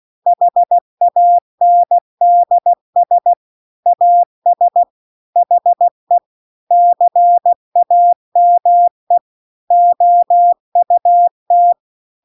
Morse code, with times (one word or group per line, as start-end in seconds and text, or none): 0.3-3.3 HANDS
3.9-4.8 AS
5.4-6.2 HE
6.7-9.2 CAME
9.7-11.7 OUT